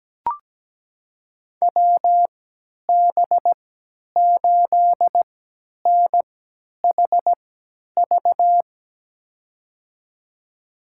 Morse code, heard as a keyed tone, about 17 wpm, spaced as for 10 wpm, 700 Hz. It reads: WB8NHV
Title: USA Call Signs - 3x Character Spacing - 17wpm